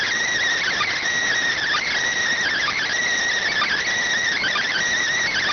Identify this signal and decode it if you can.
Emergency Alert System (SAME) tones, heard as harsh buzzing bursts. It is heard from 0.6 to 4.8 s.